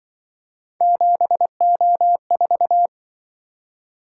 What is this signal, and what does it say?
Morse code, heard as a keyed tone, about 25 wpm, 700 Hz. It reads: 7O4